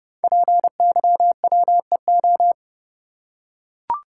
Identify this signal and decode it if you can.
Morse code, heard as a keyed tone, about 30 wpm, 700 Hz. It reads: PYWEO
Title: Code Groups - 5 Letters - 30wpm